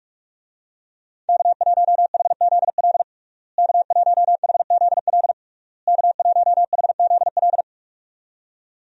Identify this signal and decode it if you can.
Morse code, heard as a keyed tone, about 46 wpm, 700 Hz. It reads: K1HZL K1HZL K1HZL
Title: USA Call Signs - 5 Characters - Repeated 3x - 45wpm